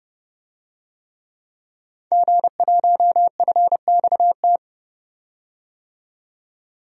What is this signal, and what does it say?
Morse code, heard as a keyed tone, about 30 wpm, 700 Hz. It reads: G1FXT
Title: DXCC-100 Call Signs - 5 Characters - 30wpm